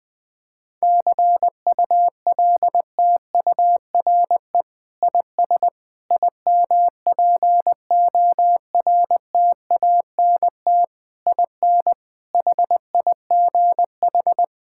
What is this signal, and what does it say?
Morse code, heard as a keyed tone, about 19 wpm, 700 Hz. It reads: CULTURE IS IMPORTANT IN HIGH
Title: Sentences from Top 1000 Words - Review Part 6 - 20wpm